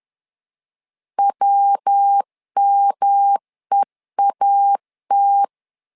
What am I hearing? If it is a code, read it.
Morse code, heard as a keyed tone, about 10 wpm, 800 Hz. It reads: WMEAT
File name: LEZIONI / 3 MWRO / 3 MW 10wpm 1959